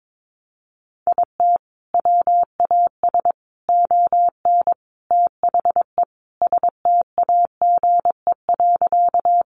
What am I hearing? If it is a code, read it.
Morse code, heard as a keyed tone, about 22 wpm, 700 Hz. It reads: IT WAS ON THE STAGE.
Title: Sentences from Top 1200 Words 22wpm